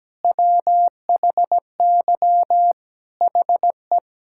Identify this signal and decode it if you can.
Morse code, heard as a keyed tone, about 17 wpm, 700 Hz. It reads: WHY HE
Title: Sentences from Top 1000 Words - Review Part 6 - 17wpm